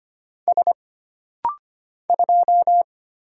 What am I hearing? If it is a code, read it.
Morse code, heard as a keyed tone, about 25 wpm, 700 Hz. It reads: S 2
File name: Single Character (Letter:Number:Punctuation:Prosign) - Rapid-Fire - 25wpm